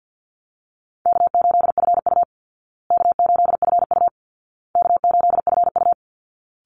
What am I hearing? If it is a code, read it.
Morse code, heard as a keyed tone, about 49 wpm, 700 Hz. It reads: K8FU K8FU K8FU
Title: Easy Call Signs Repeated 3x - 50wpm